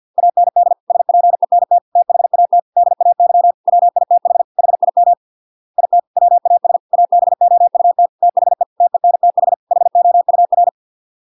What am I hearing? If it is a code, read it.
Morse code, heard as a keyed tone, about 50 wpm, 700 Hz. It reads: AND SPENT THAT DAY WITH HIM IT WAS ABOUT THE TENTH HOUR